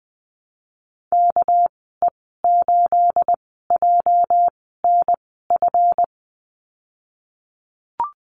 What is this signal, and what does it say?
Morse code, heard as a keyed tone, about 20 wpm, spaced as for 15 wpm, 700 Hz. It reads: KE8JNF
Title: 231 Lesson 37g - Letter x - Call Signs 20wpm